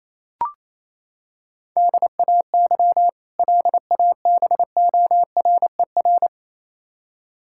Morse code, heard as a keyed tone, about 28 wpm, 700 Hz. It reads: DAY LABORER
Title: K1USN SST QSO Practice 28wpm